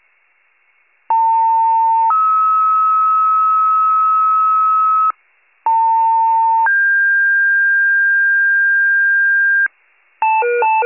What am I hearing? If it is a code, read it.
Two-tone page, heard as a one-second tone, then a three-second tone, about 900 then 1290 Hz, about 900 then 1600 Hz, with noise. It is heard from 1.1 to 9.7 s.
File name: 78 RESCUE DUTY